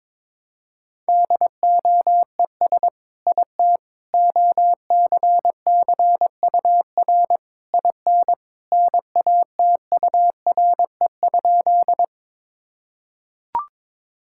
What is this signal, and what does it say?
Morse code, heard as a keyed tone, about 22 wpm, 700 Hz. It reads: DOES IT OCCUR IN NATURE?
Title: Sentences from Top 1000 Words - Review Part 6 - 22wpm